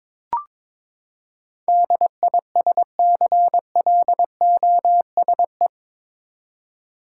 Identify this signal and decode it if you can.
Morse code, heard as a keyed tone, about 22 wpm, 700 Hz. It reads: DISCLOSE
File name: practice / 22wpm / Eight Letter Words - 22wpm